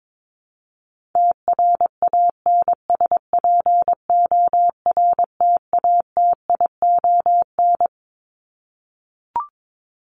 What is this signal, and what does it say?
Morse code, heard as a keyed tone, about 22 wpm, 700 Hz. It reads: TRANSPORTATION
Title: Top 1501-1750 Words 22wpm